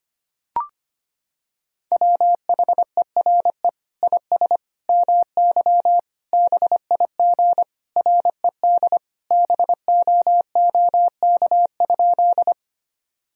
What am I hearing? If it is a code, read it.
Morse code, heard as a keyed tone, about 25 wpm, 700 Hz. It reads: WHERE IS MY BIG RED BOOK?
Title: Sentences from Top 1000 Words - Review Part 2 - 25wpm